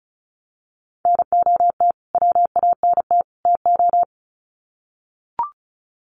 Morse code, heard as a keyed tone, about 34 wpm, 700 Hz. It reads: NOT WANT TO
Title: Sets of 3 Words 35wpm